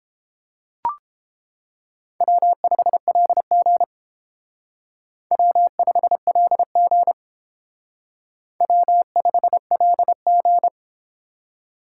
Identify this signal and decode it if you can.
Morse code, heard as a keyed tone, about 29 wpm, 700 Hz. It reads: W5LG W5LG W5LG